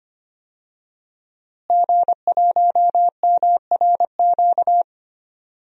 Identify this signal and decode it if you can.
Morse code, heard as a keyed tone, about 25 wpm, 700 Hz. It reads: G1MRQ
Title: DXCC-100 Call Signs - 5 Characters - Repeated 2x - 25wpm